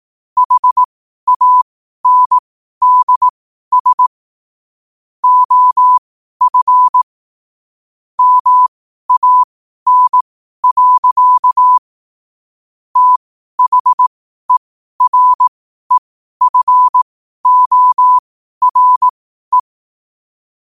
Morse code, heard as a keyed tone, about 18 wpm, spaced as for 12 wpm, 1000 Hz. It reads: HANDS OF MAN. THEREFORE
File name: qotdfiles / 12wpm